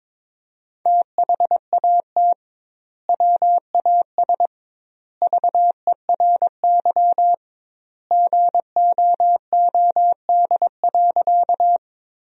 Morse code, heard as a keyed tone, about 22 wpm, 700 Hz. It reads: THAT WAS VERY GOOD.
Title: 2x Word-Spacing - Sentences from Top 200 Words - 22wpm